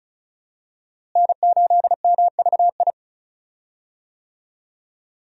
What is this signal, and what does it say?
Morse code, heard as a keyed tone, about 36 wpm, 700 Hz. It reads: N8MVI
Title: USA Call Signs - 5 Characters - Repeated 2x - 35wpm